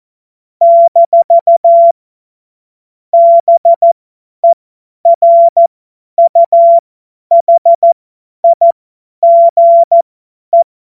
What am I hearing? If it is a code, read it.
Morse code, heard as a keyed tone, about 14 wpm, spaced as for 10 wpm, 680 Hz. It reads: - BERUHIGE
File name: Morsekurs / Witze / Witz 08. Februar 2022 50 F70